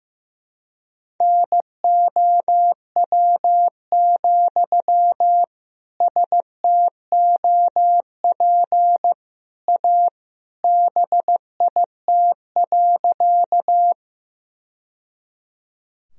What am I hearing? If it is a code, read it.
Morse code, heard as a keyed tone, about 16 wpm, 700 Hz. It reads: NOW, STOP A BIT.
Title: Sentences from Top 1000 Words - Review Part 1 - 15wpm